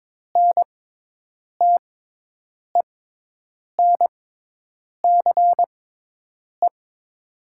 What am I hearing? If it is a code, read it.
Morse code, heard as a keyed tone, about 22 wpm, spaced as for 8 wpm, 700 Hz. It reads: NTENCE
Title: 6x Character Spacing - Top 200 Words - 22wpm